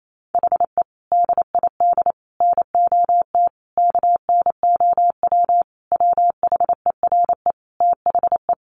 Morse code, heard as a keyed tone, about 28 wpm, 700 Hz. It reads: HE DID NOT KNOW WHERE THE
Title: Sentences from Top 1500 Words 28wpm